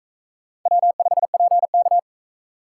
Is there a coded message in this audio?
Yes: Morse code from 0.7 to 2.0 s, heard as a keyed tone, about 37 wpm, 700 Hz. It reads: W5PK